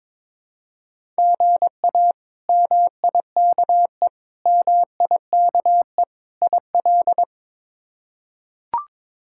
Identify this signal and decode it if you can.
Morse code, heard as a keyed tone, about 22 wpm, 700 Hz. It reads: GA MIKE MIKE IL
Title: K1USN SST QSO Practice - 22wpm